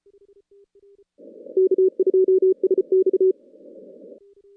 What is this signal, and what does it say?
Morse code, heard as a keyed tone, about 34 wpm, 380 Hz. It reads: K2SX
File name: contests / CW-SS-k5tr- -2100